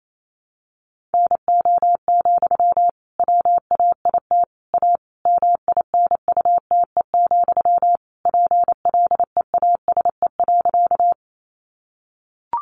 Morse code, heard as a keyed tone, about 28 wpm, 700 Hz. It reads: NO, WAIT A MINUTE, PLEASE.
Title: Sentences from Top 1500 Words 28wpm